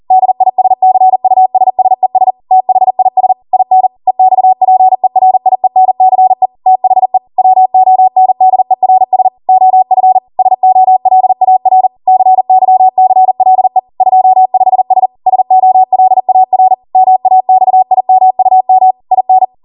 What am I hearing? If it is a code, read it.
Morse code, heard as a keyed tone, about 40 wpm, 750 Hz. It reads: DISCUSSES THIS IN EXPERIENCE THE WONDERS OF SOLAR CYCLE 25S SOLAR MAXIMUM IN